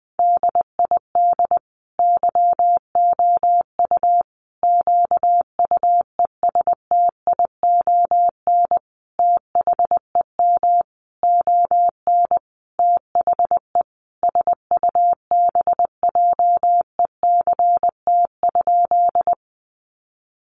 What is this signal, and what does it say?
Morse code, heard as a keyed tone, about 20 wpm, 700 Hz. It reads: DID YOU QUESTION THEM ON THE SUBJECT?
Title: Sentences from Top 800 Words 20wpm